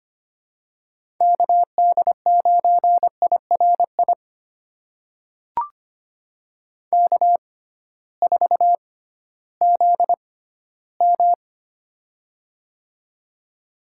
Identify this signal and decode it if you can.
Morse code, heard as a keyed tone, about 25 wpm, spaced as for 15 wpm, 700 Hz. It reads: KD9IRI K4ZM
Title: USA Call Signs - 6x Character Spacing - 25wpm